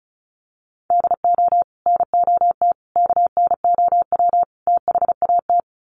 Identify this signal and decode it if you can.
Morse code, heard as a keyed tone, about 34 wpm, 700 Hz. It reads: DO NOT KNOW THAT
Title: Sets of 4 Words 35wpm